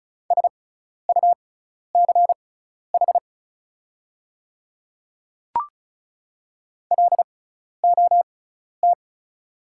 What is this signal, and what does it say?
Morse code, heard as a keyed tone, about 36 wpm, spaced as for 12 wpm, 690 Hz. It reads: SUCH LOT
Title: 6x Character Spacing - Top 500 Words - 35wpm